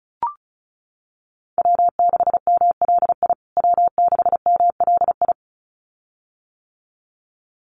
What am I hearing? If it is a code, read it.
Morse code, heard as a keyed tone, about 34 wpm, 700 Hz. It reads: W6MLI W6MLI
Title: USA Call Signs Repeated 2x 35wpm